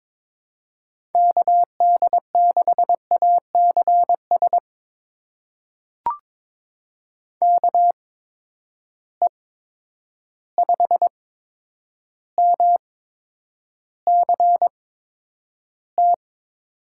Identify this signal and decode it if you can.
Morse code, heard as a keyed tone, about 22 wpm, spaced as for 9 wpm, 700 Hz. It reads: KD6ACS KE5MCT
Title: USA Call Signs - 6 Characters - 8x Character Spacing - 22wpm